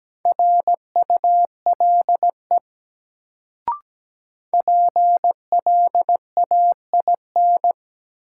Morse code, heard as a keyed tone, about 17 wpm, 700 Hz. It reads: RULE PLAIN
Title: Sentences from Top 400 Words - New Word Drill - 17wpm